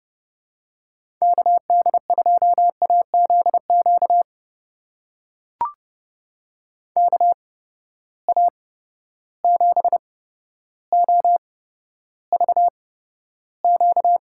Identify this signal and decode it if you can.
Morse code, heard as a keyed tone, about 31 wpm, spaced as for 13 wpm, 700 Hz. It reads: KD2AZQ KA7OVQ
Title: USA Call Signs - 8x Character Spacing - 30wpm